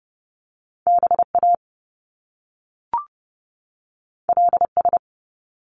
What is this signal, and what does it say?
Morse code, heard as a keyed tone, about 30 wpm, 700 Hz. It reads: BA LS
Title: Top Two Letter Combinations in Words 30wpm